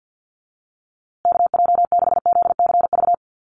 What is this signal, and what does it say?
Morse code, heard as a keyed tone, about 49 wpm, 700 Hz. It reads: KJ6ZCV
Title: USA Call Signs Repeated 3x 50wpm